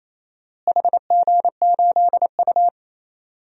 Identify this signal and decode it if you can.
Morse code, heard as a keyed tone, about 27 wpm, 700 Hz. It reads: HG8U
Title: DXCC-100 Call Signs - 4 Characters - Repeated 3x - 28wpm